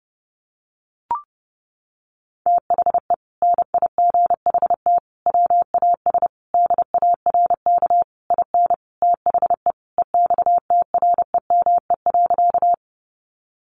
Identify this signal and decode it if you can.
Morse code, heard as a keyed tone, about 30 wpm, 700 Hz. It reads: THE NIGHT WAS DARK IN THE EXTREME.
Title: Sentences from Top 1300 Words 30wpm